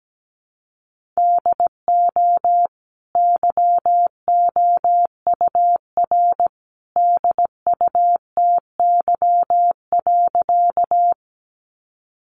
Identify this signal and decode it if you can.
Morse code, heard as a keyed tone, about 17 wpm, 700 Hz. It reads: DO YOUR DUTY.
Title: Sentences from Top 1500 Words 17wpm